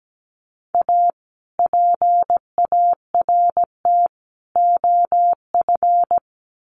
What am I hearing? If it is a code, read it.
Morse code, heard as a keyed tone, about 17 wpm, 700 Hz. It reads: A PART OF
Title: Sets of 3 Words 17wpm